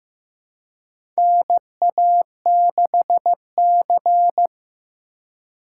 Morse code, heard as a keyed tone, about 15 wpm, 700 Hz. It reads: NA6C
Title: USA Call Signs - 4 Characters - 3x Character Spacing - 15wpm